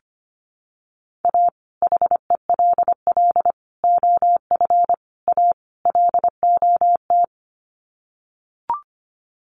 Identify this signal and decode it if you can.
Morse code, heard as a keyed tone, about 25 wpm, 700 Hz. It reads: A HELL OF A LOT
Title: Sets of 5 Words Encore 25wpm